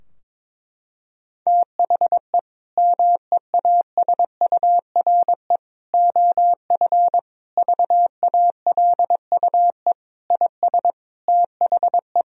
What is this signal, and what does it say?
Morse code, heard as a keyed tone, about 22 wpm, 700 Hz. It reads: THE MEASURE OF VALUE IS THE
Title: Sentences from Top 1000 Words - Review Part 3 - 22wpm